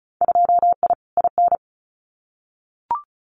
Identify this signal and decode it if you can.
Morse code, heard as a keyed tone, about 34 wpm, 700 Hz. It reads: 2I IN
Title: Winter Field Day Exchanges Encore 35wpm